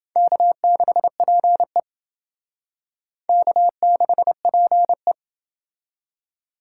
Morse code, heard as a keyed tone, about 28 wpm, 700 Hz. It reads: K6PE K6PE